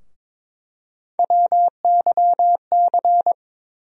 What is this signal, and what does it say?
Morse code, heard as a keyed tone, about 21 wpm, 700 Hz. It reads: WYC
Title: USA Call Sign Postfixes - 3 Letters 22wpm